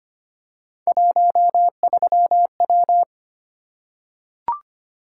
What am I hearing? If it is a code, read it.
Morse code, heard as a keyed tone, about 25 wpm, 700 Hz. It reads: 13W
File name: Three Letter-Number Combinations - 25wpm